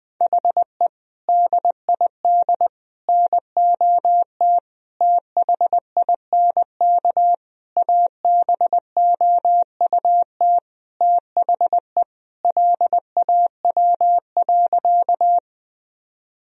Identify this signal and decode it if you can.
Morse code, heard as a keyed tone, about 20 wpm, 700 Hz. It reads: HE DID NOT THINK ABOUT THE LAW.
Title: Sentences from Top 1000 Words - Review Part 3 - 20wpm